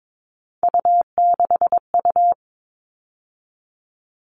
Morse code, heard as a keyed tone, about 22 wpm, 700 Hz. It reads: U6U
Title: All Cut Numbers - 3-Digit 22wpm